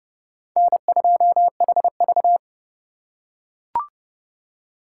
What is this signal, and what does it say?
Morse code, heard as a keyed tone, about 28 wpm, 700 Hz. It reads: N2HV